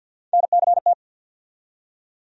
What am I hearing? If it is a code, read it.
Morse code, heard as a keyed tone, about 53 wpm, 690 Hz. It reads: NCT
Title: USA Call Sign Postfixes - 3 Letters 50wpm